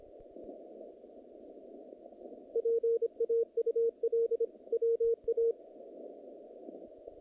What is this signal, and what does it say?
Morse code, heard as a keyed tone, about 25 wpm, 460 Hz. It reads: PAUL WA
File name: contests / naqp-cw-k5tr-op-k5pi- -2200